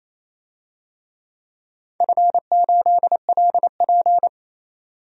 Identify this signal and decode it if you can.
Morse code, heard as a keyed tone, about 27 wpm, 690 Hz. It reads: F8LP
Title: DXCC-100 Call Signs - 4 Characters - 28wpm